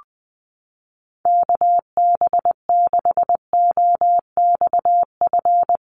Morse code, heard as a keyed tone, about 20 wpm, 700 Hz. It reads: KB6OXF
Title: USA Call Signs Repeated 2x 20wpm